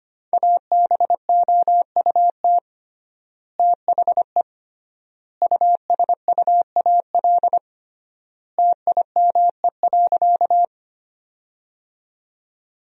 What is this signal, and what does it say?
Morse code, heard as a keyed tone, about 25 wpm, 700 Hz. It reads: ABOUT THE USUAL TIME.
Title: 3x Word-Spacing - Sentences from Top 400 Words - 25wpm